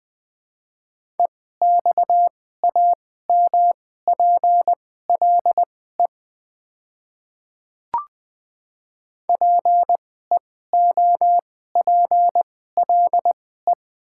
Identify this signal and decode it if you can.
Morse code, heard as a keyed tone, about 20 wpm, spaced as for 14 wpm, 700 Hz. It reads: EXAMPLE PEOPLE
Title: 2x Character Spacing - Top 300 Words - 20wpm